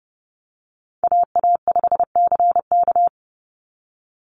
Morse code, heard as a keyed tone, about 31 wpm, 700 Hz. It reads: AA5CK
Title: NA SOTA Chaser Callsigns 30wpm